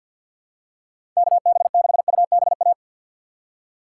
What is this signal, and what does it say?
Morse code, heard as a keyed tone, about 52 wpm, 700 Hz. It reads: KB6UBA
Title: USA Call Signs - 6 Characters - 6x Character Spacing - 50wpm